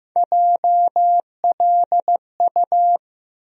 Morse code, heard as a keyed tone, about 15 wpm, 700 Hz. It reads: JLU
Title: USA Call Sign Postfixes - 3 Letters 15wpm